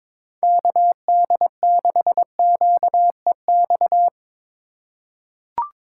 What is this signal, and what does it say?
Morse code, heard as a keyed tone, about 22 wpm, 700 Hz. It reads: KD6QEX
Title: USA Call Signs - 6 Characters - 3x Character Spacing - 22wpm